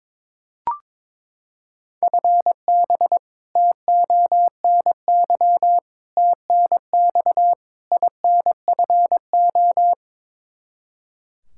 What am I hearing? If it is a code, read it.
Morse code, heard as a keyed tone, about 22 wpm, 700 Hz. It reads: FB TONY TNX INFO
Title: Practice QSO - Exchange 3 - More Info - 22wpm